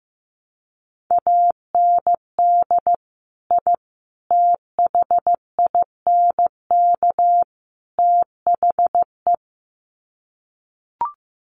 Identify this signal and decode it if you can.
Morse code, heard as a keyed tone, about 15 wpm, 700 Hz. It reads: AND I THINK THE
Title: Sets of 4 Words 15wpm